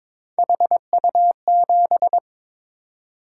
Morse code, heard as a keyed tone, about 22 wpm, 700 Hz. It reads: HU7
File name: Three Letter-Number Combinations - 22wpm